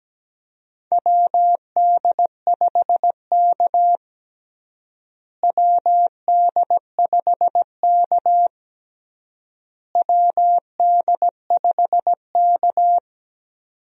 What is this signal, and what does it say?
Morse code, heard as a keyed tone, about 17 wpm, 700 Hz. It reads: WD5K WD5K WD5K
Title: USA Call Signs - 4 Characters - Repeated 3x - 17wpm